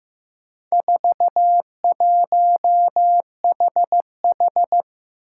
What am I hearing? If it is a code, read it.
Morse code, heard as a keyed tone, about 16 wpm, 700 Hz. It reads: 41HH
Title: Four Challenging Letter-Number Combinations 15wpm